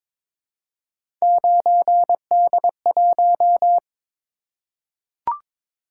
Morse code, heard as a keyed tone, about 22 wpm, 700 Hz. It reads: 9D1
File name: practice / 22wpm / Three Letter-Number Combinations - 22wpm